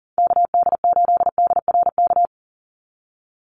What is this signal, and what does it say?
Morse code, heard as a keyed tone, about 39 wpm, 690 Hz. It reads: KD8DRK
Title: USA Call Signs 40wpm